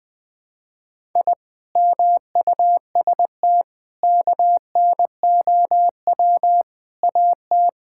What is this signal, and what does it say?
Morse code, heard as a keyed tone, about 20 wpm, 700 Hz. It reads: I MUST KNOW AT
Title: Sentences from Top 1000 Words - Review Part 5 - 20wpm